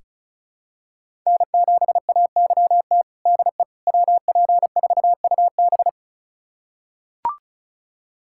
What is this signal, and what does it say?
Morse code, heard as a keyed tone, about 34 wpm, 700 Hz. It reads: N7AYT DE WP4UB
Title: USA Callsign DE Callsign - 5 Characters - 35wpm